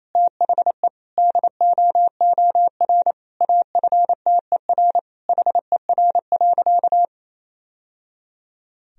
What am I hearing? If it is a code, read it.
Morse code, heard as a keyed tone, about 28 wpm, 700 Hz. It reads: THE DOOR AFTER HER.
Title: Sentences from Top 1000 Words - Review Part 1 - 28wpm